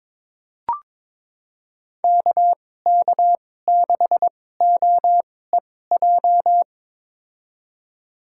Morse code, heard as a keyed tone, about 22 wpm, spaced as for 16 wpm, 700 Hz. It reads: KK6OEJ
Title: USA Call Signs - 6 Characters - 2x Character Spacing - 22wpm